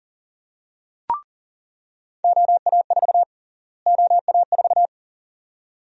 Morse code, heard as a keyed tone, about 37 wpm, 700 Hz. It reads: OA4 OA4
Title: USA Call Sign Prefixes - Repeated 2x - 40wpm